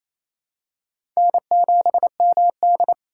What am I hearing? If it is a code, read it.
Morse code, heard as a keyed tone, about 28 wpm, 700 Hz. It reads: N7MD
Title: USA Call Signs - 4 Characters - 2x Character Spacing - 28wpm